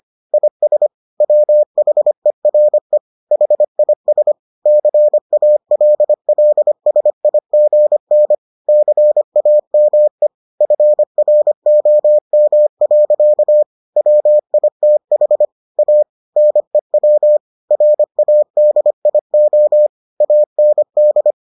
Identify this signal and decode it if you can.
Morse code, heard as a keyed tone, about 25 wpm, 600 Hz. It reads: IS WHERE HIS CALLSIGN CAME FROM. WITH A NEW RADIO AND